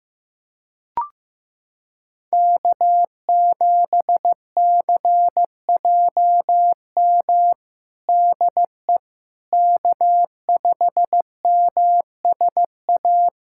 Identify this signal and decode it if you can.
Morse code, heard as a keyed tone, about 16 wpm, 700 Hz. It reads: K7CJM DE K5MSA
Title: USA Callsign DE Callsign - 5 Characters - 15wpm